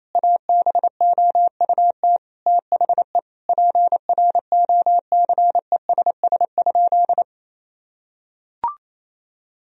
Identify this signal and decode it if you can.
Morse code, heard as a keyed tone, about 28 wpm, 700 Hz. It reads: ABOUT THE PROCESS?
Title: Sentences from Top 1000 Words - Review Part 3 - 28wpm